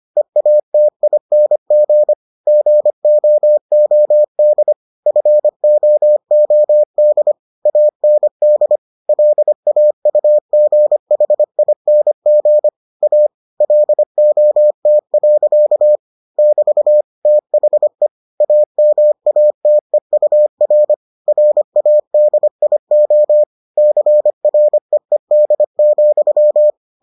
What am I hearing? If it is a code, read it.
Morse code, heard as a keyed tone, about 25 wpm, 600 Hz. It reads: EATING GOOD FOOD AND LAUGHING A LOT. <BT> THE AMATEUR RADIO CREED,